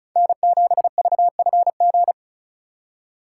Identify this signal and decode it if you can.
Morse code, heard as a keyed tone, about 44 wpm, 700 Hz. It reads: N7VFG